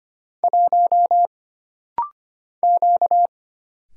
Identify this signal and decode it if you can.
Morse code, heard as a keyed tone, about 25 wpm, 700 Hz. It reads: 1 Q
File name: Single Character (Letter:Number:Punctuation:Prosign) - Rapid-Fire - 25wpm